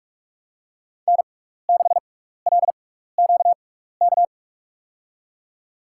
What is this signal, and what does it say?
Morse code, heard as a keyed tone, about 44 wpm, spaced as for 15 wpm, 700 Hz. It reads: N6LQK
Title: USA Call Signs - 6x Character Spacing - 45wpm